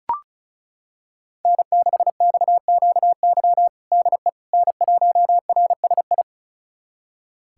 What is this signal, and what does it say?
Morse code, heard as a keyed tone, about 36 wpm, 700 Hz. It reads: N6XQY DE N1RSI
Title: USA Callsign DE Callsign - 5 Characters - 35wpm